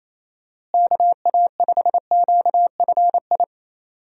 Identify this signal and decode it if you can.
Morse code, heard as a keyed tone, about 28 wpm, 700 Hz. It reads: KA5QFI
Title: USA Call Signs - 6 Characters - Repeated 3x - 28wpm